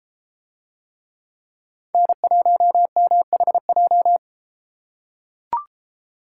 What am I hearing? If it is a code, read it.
Morse code, heard as a keyed tone, about 31 wpm, 700 Hz. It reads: N1MHJ